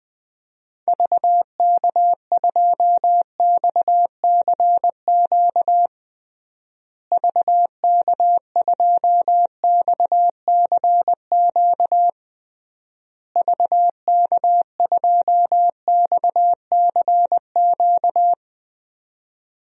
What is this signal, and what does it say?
Morse code, heard as a keyed tone, about 20 wpm, 700 Hz. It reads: VK2XCQ VK2XCQ VK2XCQ
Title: DXCC-100 Call Signs - 6 Characters - Repeated 3x - 20wpm